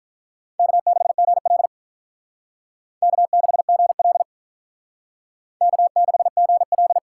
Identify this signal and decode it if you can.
Morse code, heard as a keyed tone, about 45 wpm, 700 Hz. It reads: K6GL K6GL K6GL